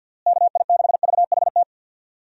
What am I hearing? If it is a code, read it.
Morse code, heard as a keyed tone, about 50 wpm, 700 Hz. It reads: KI6VHT
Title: USA Call Signs - 6 Characters - Repeated 3x - 50wpm